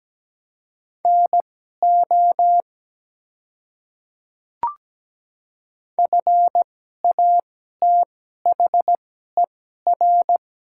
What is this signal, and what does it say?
Morse code, heard as a keyed tone, about 17 wpm, spaced as for 12 wpm, 700 Hz. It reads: NO FATHER
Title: 2x Character Spacing - Top 200 Words - 17wpm